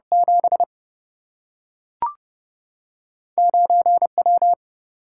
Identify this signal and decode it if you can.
Morse code, heard as a keyed tone, about 29 wpm, 700 Hz. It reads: 7 9W
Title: Single-Letter Single-Number 30wpm